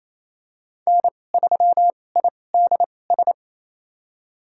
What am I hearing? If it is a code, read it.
Morse code, heard as a keyed tone, about 27 wpm, spaced as for 20 wpm, 700 Hz. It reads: N3IDS
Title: USA Call Signs - 2x Character Spacing - 28wpm